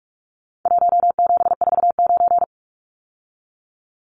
Morse code, heard as a keyed tone, about 44 wpm, 700 Hz. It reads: 1749
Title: Numbers - Contest Style - 4-Digit 45wpm